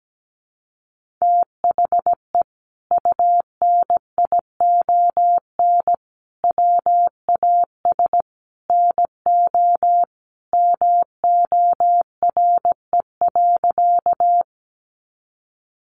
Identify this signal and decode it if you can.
Morse code, heard as a keyed tone, about 17 wpm, 700 Hz. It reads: THE UNION WAS NO MORE.
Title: Sentences from Top 1400 Words 17wpm